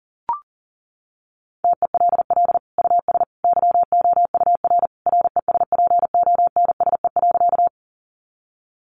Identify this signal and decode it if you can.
Morse code, heard as a keyed tone, about 37 wpm, 700 Hz. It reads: TELL US YOUR RESPONSE.
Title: Sentences from Top 1400 Words 40wpm